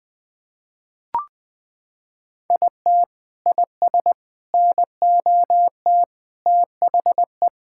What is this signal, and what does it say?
Morse code, heard as a keyed tone, about 20 wpm, 700 Hz. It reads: IT IS NOT THE
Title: Sentences from Top 1000 Words - Review Part 6 - 20wpm